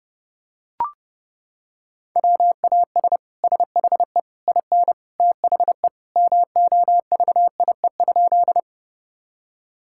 Morse code, heard as a keyed tone, about 30 wpm, 700 Hz. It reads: WAS SHE IN THE MOVIE?
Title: Sentences from Top 1000 Words - Review Part 5 - 30wpm